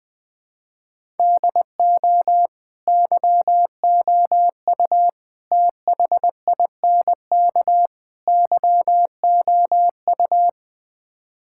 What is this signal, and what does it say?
Morse code, heard as a keyed tone, about 20 wpm, 700 Hz. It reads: DO YOU THINK YOU
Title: Build A Phrase - 4-Word Phrases - 20wpm